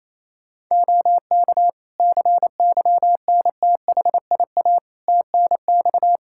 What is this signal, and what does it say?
Morse code, heard as a keyed tone, about 27 wpm, 700 Hz. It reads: OK CYNTHIA TNX
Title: K1USN SST QSO Practice 28wpm